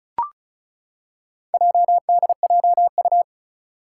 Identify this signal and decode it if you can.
Morse code, heard as a keyed tone, about 36 wpm, 700 Hz. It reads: JDJU
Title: Four Challenging Letter-Number Combinations 35wpm